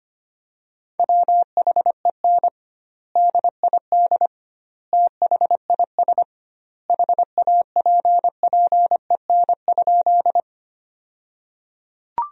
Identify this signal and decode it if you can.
Morse code, heard as a keyed tone, about 25 wpm, 700 Hz. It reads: WHEN DID THIS HAPPEN?
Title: 2x Word-Spacing - Sentences from Top 400 Words - 25wpm